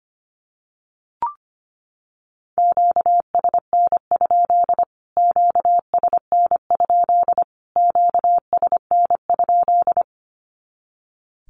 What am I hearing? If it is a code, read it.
Morse code, heard as a keyed tone, about 25 wpm, 700 Hz. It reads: QSN? QSN? QSN?
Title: Q-Codes Repeated 3x 25wpm